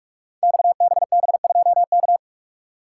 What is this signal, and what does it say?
Morse code, heard as a keyed tone, about 44 wpm, 700 Hz. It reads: XBB2K
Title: Code Groups - 5 Letter:Numbers - 45wpm